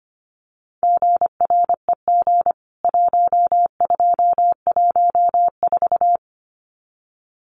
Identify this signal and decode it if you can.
Morse code, heard as a keyed tone, about 25 wpm, 700 Hz. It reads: GREG 1214
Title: CWT Exchange 25wpm